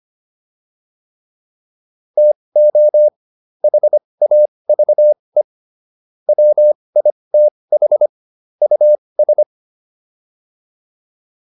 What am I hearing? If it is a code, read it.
Morse code, heard as a keyed tone, about 25 wpm, spaced as for 20 wpm, 600 Hz. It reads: TO HAVE WITH US